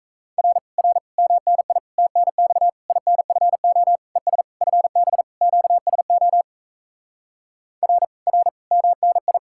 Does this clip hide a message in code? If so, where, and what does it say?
Yes: Morse code from 0.4 to 9.4 s, heard as a keyed tone, about 37 wpm, 700 Hz. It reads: R R MNI TNX INFO ES FB QSO R R MNI